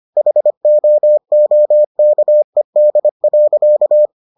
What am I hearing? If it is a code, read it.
Morse code, heard as a keyed tone, about 25 wpm, 600 Hz. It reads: HOOKED.